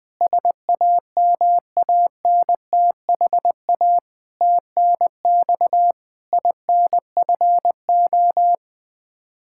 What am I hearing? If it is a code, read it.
Morse code, heard as a keyed tone, about 20 wpm, 700 Hz. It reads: SAMANTHA TNX INFO
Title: K1USN SST QSO Practice 20wpm